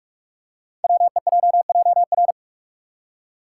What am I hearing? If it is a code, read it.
Morse code, heard as a keyed tone, about 44 wpm, 700 Hz. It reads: WEJJR
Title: Code Groups - 5 Letters - 45wpm